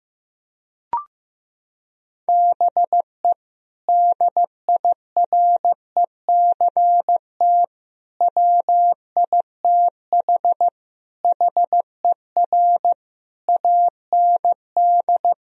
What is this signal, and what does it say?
Morse code, heard as a keyed tone, about 16 wpm, 700 Hz. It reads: BE DIRECT WITH HER AND
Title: Sentences from Top 1000 Words - Review Part 2 - 15wpm